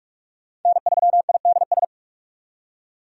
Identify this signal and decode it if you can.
Morse code, heard as a keyed tone, about 46 wpm, 700 Hz. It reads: N3IDS
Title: USA Call Signs - 5 Characters - 45wpm